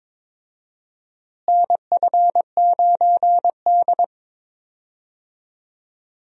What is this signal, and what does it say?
Morse code, heard as a keyed tone, about 22 wpm, 700 Hz. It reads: NF9D
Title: USA Call Signs - 4 Characters - Repeated 2x - 22wpm